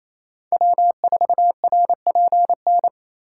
Morse code, heard as a keyed tone, about 28 wpm, 700 Hz. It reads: W4RPN
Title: USA Call Signs - 5 Characters - 8x Character Spacing - 28wpm